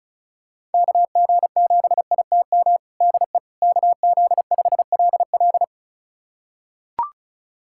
Morse code, heard as a keyed tone, about 34 wpm, 700 Hz. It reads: KG7ITM DE KZ5LL
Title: USA Callsign DE Callsign 35wpm